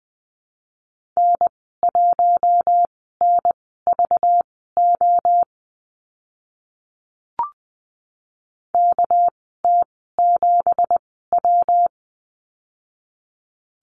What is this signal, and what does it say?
Morse code, heard as a keyed tone, about 20 wpm, spaced as for 14 wpm, 700 Hz. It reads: N1NVO KT7W
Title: 238 Lesson 38g - Letter q - Call Signs 20wpm